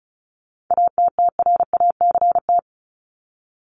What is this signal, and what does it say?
Morse code, heard as a keyed tone, about 34 wpm, 700 Hz. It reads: ATTRACT
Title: Top 1501-1750 Words 35wpm